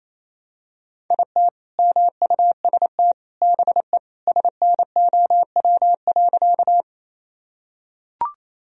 Morse code, heard as a keyed tone, about 28 wpm, 700 Hz. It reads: IT MUST BE SNOW.
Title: Sentences from Top 1000 Words - Review Part 3 - 28wpm